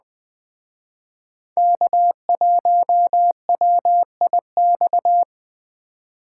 Morse code, heard as a keyed tone, about 21 wpm, 700 Hz. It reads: K1WIX
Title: Speed Racing - USA Call Signs 17wpm